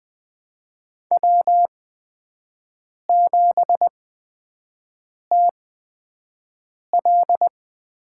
Morse code, heard as a keyed tone, about 20 wpm, 700 Hz. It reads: W 7 T L
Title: USA Call Signs - 5 Characters - 8x Character Spacing - 20wpm